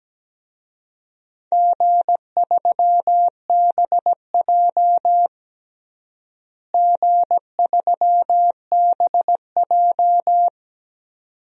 Morse code, heard as a keyed tone, about 17 wpm, 700 Hz. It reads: G3BJ G3BJ
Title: DXCC-100 Call Signs - 4 Characters - Repeated 2x - 17wpm